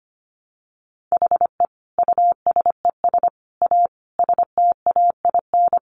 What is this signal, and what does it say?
Morse code, heard as a keyed tone, about 25 wpm, 700 Hz. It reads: HE USES A STAIN
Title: Additional Easy Phrases 25wpm